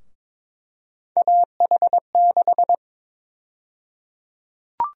Morse code, heard as a keyed tone, about 22 wpm, 700 Hz. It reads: AH6
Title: USA Call Sign Prefixes - Repeated 2x - 22wpm